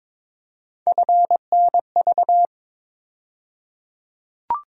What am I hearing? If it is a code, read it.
Morse code, heard as a keyed tone, about 22 wpm, 700 Hz. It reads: FNV
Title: USA Call Sign Postfixes - 3 Letters 22wpm